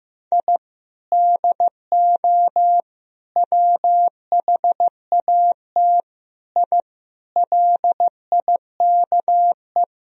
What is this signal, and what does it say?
Morse code, heard as a keyed tone, about 15 wpm, 700 Hz. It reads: I DO WHAT I LIKE
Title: Sentences from Top 1000 Words - Review Part 2 - 15wpm